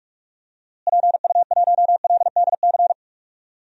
Morse code, heard as a keyed tone, about 41 wpm, 700 Hz. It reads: PU1LDC
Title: DXCC-100 Call Signs - 6 Characters - Repeated 2x - 45wpm